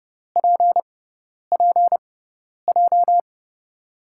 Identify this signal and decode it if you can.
Morse code, heard as a keyed tone, about 32 wpm, spaced as for 10 wpm, 700 Hz. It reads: PPJ
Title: USA Call Signs - 5 Characters - 6x Character Spacing - 30wpm